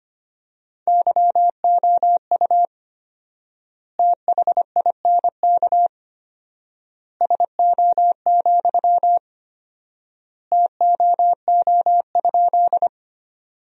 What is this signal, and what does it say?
Morse code, heard as a keyed tone, about 25 wpm, 700 Hz. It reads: YOU THINK SO, TOO?
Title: 4x Word-Spacing - Sentences from Top 200 Words - 25wpm